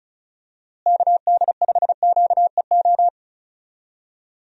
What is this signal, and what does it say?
Morse code, heard as a keyed tone, about 36 wpm, 700 Hz. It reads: KD5QEO
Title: USA Call Signs - 6 Characters - Repeated 3x - 35wpm